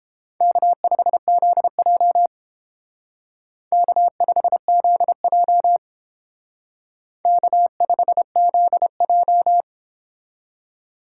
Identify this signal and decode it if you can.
Morse code, heard as a keyed tone, about 31 wpm, 700 Hz. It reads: K5ZJ K5ZJ K5ZJ